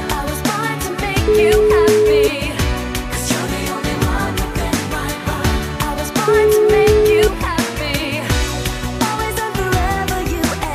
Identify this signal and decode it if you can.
Ringback tone (Europe, 425 Hz), with noise.